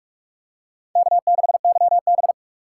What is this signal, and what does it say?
Morse code, heard as a keyed tone, about 44 wpm, 700 Hz. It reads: K6YB